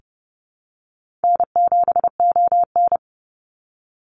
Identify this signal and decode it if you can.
Morse code, heard as a keyed tone, about 30 wpm, 700 Hz. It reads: N7ON
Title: Call Signs Repeated 3x - 30wpm